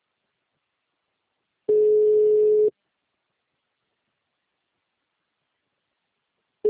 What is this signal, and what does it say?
Ringback tone (Europe, 425 Hz).
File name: out- -2020